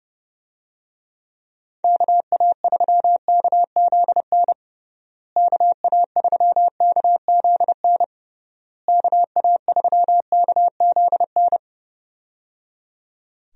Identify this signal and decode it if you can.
Morse code, heard as a keyed tone, about 30 wpm, 700 Hz. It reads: KA3KZN KA3KZN KA3KZN
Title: USA Call Signs - 6 Characters - Repeated 3x - 30wpm